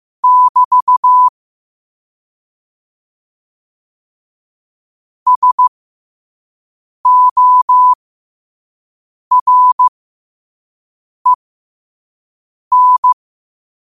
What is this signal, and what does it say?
Morse code, heard as a keyed tone, about 15 wpm, spaced as for 5 wpm, 1000 Hz. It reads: <BT> SOREN
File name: qotdfiles / 5wpm